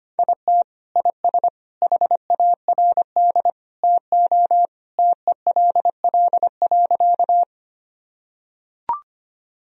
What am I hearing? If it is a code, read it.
Morse code, heard as a keyed tone, about 25 wpm, 700 Hz. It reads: IT IS HARD TO TELL.
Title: Sentences from Top 1000 Words - Review Part 1 - 25wpm